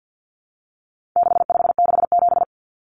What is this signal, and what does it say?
Morse code, heard as a keyed tone, about 50 wpm, 700 Hz. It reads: 6567
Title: Numbers - 4-Digit 50wpm